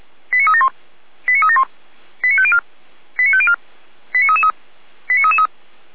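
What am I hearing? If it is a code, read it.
Five-tone selective call (ZVEI), heard as quick runs of five tones, tones of about 70 ms. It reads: 89371 89371 89504 89504 89333 89333
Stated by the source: Five-tone ZVEI selective call received by SDR, from the fire brigade Feuerwehr Orlamünde.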